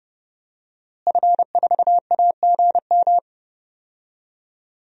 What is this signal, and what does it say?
Morse code, heard as a keyed tone, about 30 wpm, 700 Hz. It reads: F4AGM
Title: DXCC-100 Call Signs - 5 Characters - Repeated 3x - 30wpm